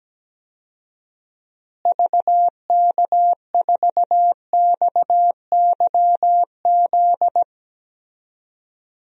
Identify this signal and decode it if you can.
Morse code, heard as a keyed tone, about 17 wpm, 700 Hz. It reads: VK4XYZ
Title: DXCC-100 Call Signs - 6 Characters - Repeated 3x - 17wpm